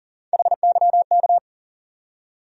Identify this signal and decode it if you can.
Morse code, heard as a keyed tone, about 40 wpm, 700 Hz. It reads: HYK
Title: USA Call Sign Postfixes - 3 Letters 40wpm